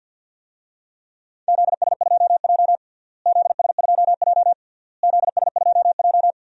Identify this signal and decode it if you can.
Morse code, heard as a keyed tone, about 50 wpm, 700 Hz. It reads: ZS2J ZS2J ZS2J
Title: DXCC-100 Call Signs - 4 Characters - Repeated 3x - 50wpm